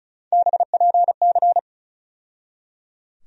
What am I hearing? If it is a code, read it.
Morse code, heard as a keyed tone, about 34 wpm, 700 Hz. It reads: BPC
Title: USA Call Sign Postfixes - 3 Letters 35wpm